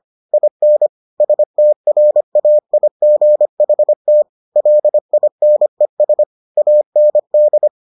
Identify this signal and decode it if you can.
Morse code, heard as a keyed tone, about 25 wpm, 600 Hz. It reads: IN STRAIGHT LINES AND